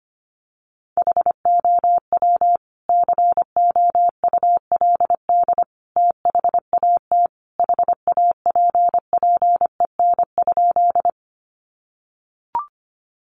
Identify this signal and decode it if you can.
Morse code, heard as a keyed tone, about 25 wpm, 700 Hz. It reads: HOW COULD THAT HAPPEN?
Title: Sentences from Top 400 Words 25wpm